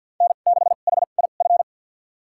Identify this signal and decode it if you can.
Morse code, heard as a keyed tone, about 53 wpm, spaced as for 37 wpm, 690 Hz. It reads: N6HIF
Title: USA Call Signs - 5 Characters - 2x Character Spacing - 50wpm